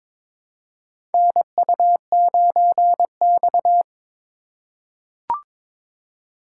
Morse code, heard as a keyed tone, about 22 wpm, 700 Hz. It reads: NU9X
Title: USA Call Signs - 4 Characters - 8x Character Spacing - 22wpm